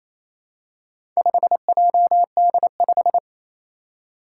Morse code, heard as a keyed tone, about 27 wpm, 700 Hz. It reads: 5JD5
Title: Four Challenging Letter-Number Combinations 28wpm